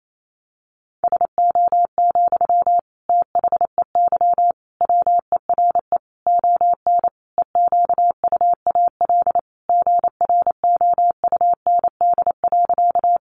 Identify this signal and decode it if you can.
Morse code, heard as a keyed tone, about 28 wpm, 700 Hz. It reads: SO, THEY WERE ON EQUAL GROUND.
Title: Sentences from Top 1400 Words 28wpm